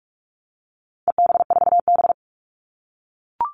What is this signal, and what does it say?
Morse code, heard as a keyed tone, about 44 wpm, 700 Hz. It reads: EB4B
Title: All Cut Numbers - 4-Digit 45wpm